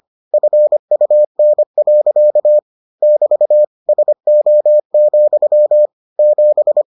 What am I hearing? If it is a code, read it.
Morse code, heard as a keyed tone, about 25 wpm, 600 Hz. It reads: FUN. <BT> SO, 7